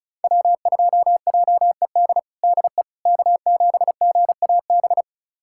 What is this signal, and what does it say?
Morse code, heard as a keyed tone, about 36 wpm, 700 Hz. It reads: W2JED DE K7GAB
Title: USA Callsign DE Callsign - 5 Characters - 35wpm